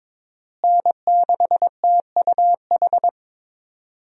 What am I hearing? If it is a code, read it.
Morse code, heard as a keyed tone, about 22 wpm, 700 Hz. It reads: N6TUH
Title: USA Call Signs - 5 Characters - Repeated 3x - 22wpm